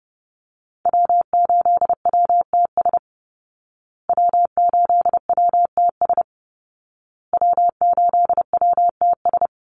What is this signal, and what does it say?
Morse code, heard as a keyed tone, about 30 wpm, 700 Hz. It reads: W8WTS W8WTS W8WTS
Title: Call Signs Repeated 2x - 30wpm